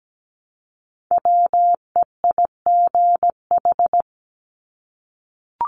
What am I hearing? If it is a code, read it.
Morse code, heard as a keyed tone, about 17 wpm, 700 Hz. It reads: WEIGH
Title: Top 1751-2000 Words 17wpm